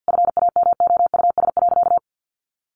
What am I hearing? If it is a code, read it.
Morse code, heard as a keyed tone, about 50 wpm, 700 Hz. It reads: FAMOUS.
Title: Sentences from Top 1500 Words 50wpm